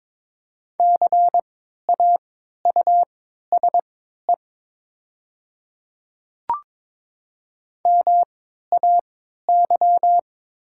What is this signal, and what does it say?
Morse code, heard as a keyed tone, about 22 wpm, spaced as for 13 wpm, 700 Hz. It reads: CAUSE MAY
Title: 3x Character Spacing - Top 200 Words - 22wpm